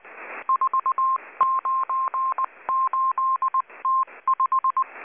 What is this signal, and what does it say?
Morse code, heard as a keyed tone, about 19 wpm, 1060 Hz. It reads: 498T5
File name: HM02- -0540utc